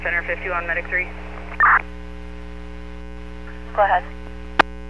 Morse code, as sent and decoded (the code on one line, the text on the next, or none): none